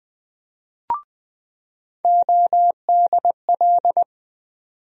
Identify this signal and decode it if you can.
Morse code, heard as a keyed tone, about 20 wpm, 700 Hz. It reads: ODL
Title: USA Call Sign Postfixes - 3 Letters 20wpm